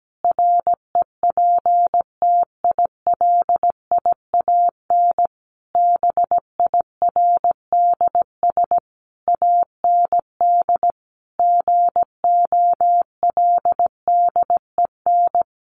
Morse code, heard as a keyed tone, about 17 wpm, 700 Hz. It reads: REPTILIAN BIRDS AND GOLDEN
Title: A Sound Of Thunder 17wpm